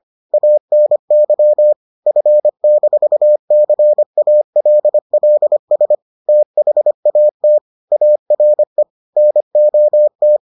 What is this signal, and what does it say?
Morse code, heard as a keyed tone, about 25 wpm, 600 Hz. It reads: ANY F-CALLS THAT ARE NOT